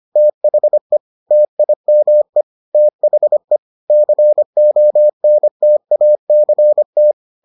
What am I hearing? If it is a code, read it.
Morse code, heard as a keyed tone, about 25 wpm, 600 Hz. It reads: THE TIME THE CONTACT